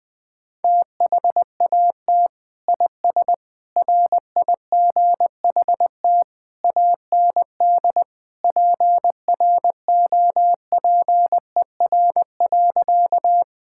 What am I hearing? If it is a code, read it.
Morse code, heard as a keyed tone, about 20 wpm, 700 Hz. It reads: THAT IS RIGHT AND PROPER.
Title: Sentences from Top 1000 Words - Review Part 5 - 20wpm